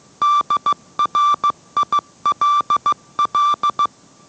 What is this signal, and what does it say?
Morse code, heard as a keyed tone, about 15 wpm, 1170 Hz. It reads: DRILL